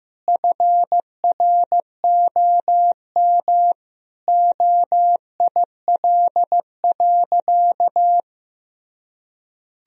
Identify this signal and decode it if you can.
Morse code, heard as a keyed tone, about 15 wpm, 700 Hz. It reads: FROM OIL.
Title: Sentences from Top 1000 Words - Review Part 4 - 15wpm